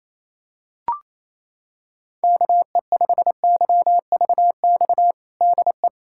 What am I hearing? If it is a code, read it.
Morse code, heard as a keyed tone, about 28 wpm, 700 Hz. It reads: KE5YVX DE
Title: USA Callsign DE Callsign - 6 Characters - 28wpm